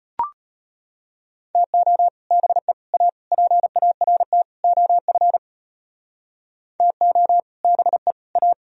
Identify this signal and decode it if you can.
Morse code, heard as a keyed tone, about 30 wpm, 700 Hz. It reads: TO BE A PART OF TO BE A
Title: Speed-Racing - Sets of 5 Words 25wpm